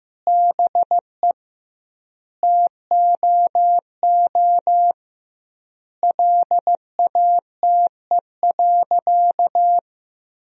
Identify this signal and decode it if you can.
Morse code, heard as a keyed tone, about 15 wpm, 700 Hz. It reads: BE TOO LATE.
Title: 2x Word-Spacing - Sentences from Top 300 Words - 15wpm